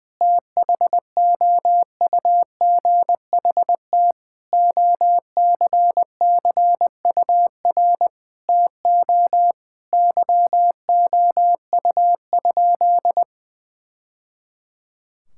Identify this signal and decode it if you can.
Morse code, heard as a keyed tone, about 20 wpm, 700 Hz. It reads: THOUGHT OCCUR TO YOU?
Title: Sentences from Top 1000 Words - Review Part 6 - 20wpm